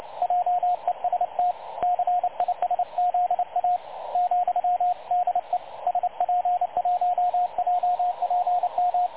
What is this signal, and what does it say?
Morse code, heard as a keyed tone, about 28 wpm, 720 Hz, with radio noise. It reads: JEST CISZA ,DE SP1JPM
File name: dlazaawansowanych / nagraniazpasma / websdr recording 3565.0kHz